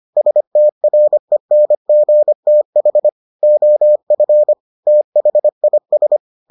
Morse code, heard as a keyed tone, about 25 wpm, 600 Hz. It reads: STRENGTH OF THIS